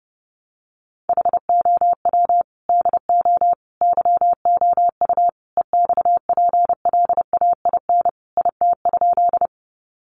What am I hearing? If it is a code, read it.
Morse code, heard as a keyed tone, about 30 wpm, 700 Hz. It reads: HOW DO YOU EXPLAIN IT?
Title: Sentences from Top 1100 Words 30wpm